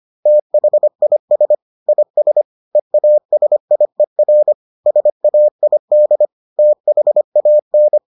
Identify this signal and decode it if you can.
Morse code, heard as a keyed tone, about 25 wpm, 600 Hz. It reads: THIS IS EASIER SAID THAN